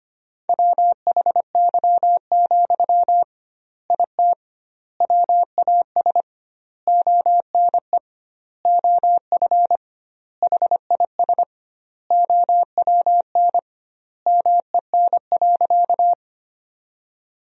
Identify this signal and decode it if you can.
Morse code, heard as a keyed tone, about 25 wpm, 700 Hz. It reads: WHY, IT WAS ONE OF HIS OWN MEN.
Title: 2x Word-Spacing - Sentences from Top 300 Words - 25wpm